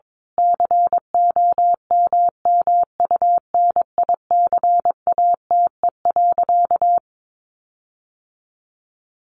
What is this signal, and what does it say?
Morse code, heard as a keyed tone, about 22 wpm, 700 Hz. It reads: COMMUNICATE.
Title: Sentences from Top 1300 Words 22wpm